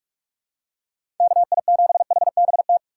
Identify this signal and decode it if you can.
Morse code, heard as a keyed tone, about 44 wpm, 700 Hz. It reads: KI7HBT
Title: USA Call Signs - 6 Characters - 6x Character Spacing - 45wpm